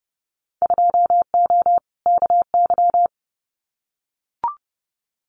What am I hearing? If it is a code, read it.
Morse code, heard as a keyed tone, about 30 wpm, 700 Hz. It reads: 2O KY
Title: Winter Field Day Exchanges 30wpm